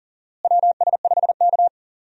Morse code, heard as a keyed tone, about 40 wpm, 700 Hz. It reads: WS5K
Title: USA Call Signs - 4 Characters - Repeated 2x - 40wpm